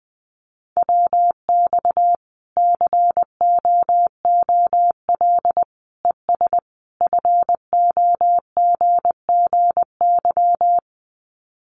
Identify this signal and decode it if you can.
Morse code, heard as a keyed tone, about 20 wpm, 700 Hz. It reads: WX COOL ES FOGGY
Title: Common QSO Phrases 20wpm